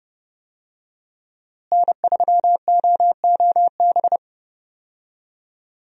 Morse code, heard as a keyed tone, about 30 wpm, 700 Hz. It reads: N3OOB
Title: USA Call Signs - 5 Characters - Repeated 3x - 30wpm